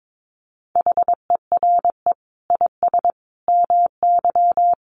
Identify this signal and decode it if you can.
Morse code, heard as a keyed tone, about 22 wpm, 700 Hz. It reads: HERE IS MY
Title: Sentences from Top 1400 Words 22wpm